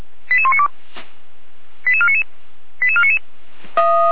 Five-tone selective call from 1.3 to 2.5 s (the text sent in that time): 88499